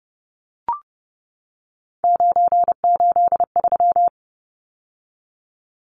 Morse code, heard as a keyed tone, about 30 wpm, 700 Hz. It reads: 983
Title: Numbers - 3-Digit 30wpm